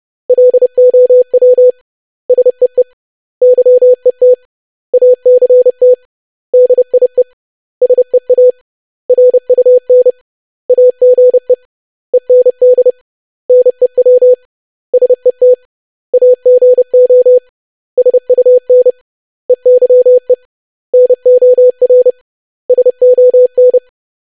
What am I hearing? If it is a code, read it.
Morse code, heard as a keyed tone, about 30 wpm, 500 Hz. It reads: LOW SEE YET ACT DIE SEA RUN AGE END NEW SET AGO SUN EYE NOR SON
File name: lesson / web / basic500-25wpm